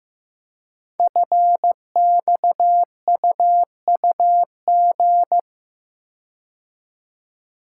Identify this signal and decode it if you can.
Morse code, heard as a keyed tone, about 15 wpm, 700 Hz. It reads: FXUUG
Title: Code Groups - 5 Letters - 15wpm